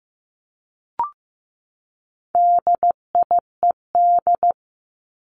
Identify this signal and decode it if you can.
Morse code, heard as a keyed tone, about 15 wpm, 700 Hz. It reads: DIED
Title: ED Suffix - Top 100 Words with Base Words 15wpm